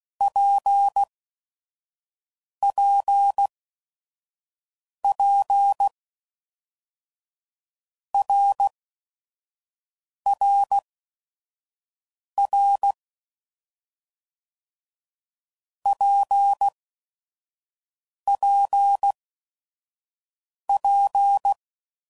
Morse code, heard as a keyed tone, about 16 wpm, spaced as for 5 wpm, 800 Hz. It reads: PPP RRR PPP
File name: Basic Letter Groups By7Groups / RKPX / RKPX@5 f15